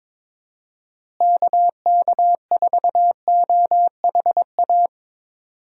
Morse code, heard as a keyed tone, about 22 wpm, 700 Hz. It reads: KK4OHA
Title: USA Call Signs - 6 Characters - Repeated 3x - 22wpm